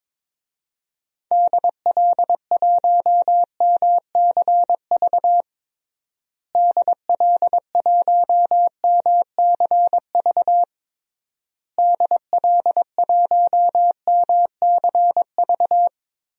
Morse code, heard as a keyed tone, about 22 wpm, 700 Hz. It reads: DL1MCV DL1MCV DL1MCV
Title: DXCC-100 Call Signs - 6 Characters - Repeated 3x - 22wpm